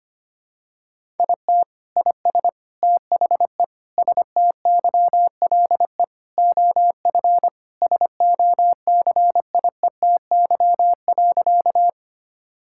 Morse code, heard as a keyed tone, about 25 wpm, 700 Hz. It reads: IT IS THE STYLE OF SOCIETY.
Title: Sentences from Top 1000 Words - Review Part 5 - 25wpm